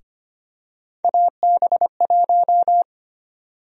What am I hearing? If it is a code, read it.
Morse code, heard as a keyed tone, about 25 wpm, 700 Hz. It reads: AB1
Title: USA Call Sign Prefixes - Repeated 2x - 25wpm